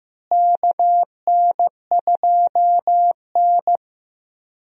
Morse code, heard as a keyed tone, about 15 wpm, 700 Hz. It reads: KN2N
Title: USA Call Signs - 4 Characters - Repeated 3x - 15wpm